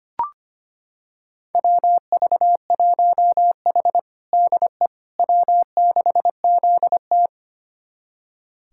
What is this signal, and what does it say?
Morse code, heard as a keyed tone, about 25 wpm, 700 Hz. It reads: WV1H DE W6ZT
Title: USA Callsign DE Callsign - 4 Characters - 25wpm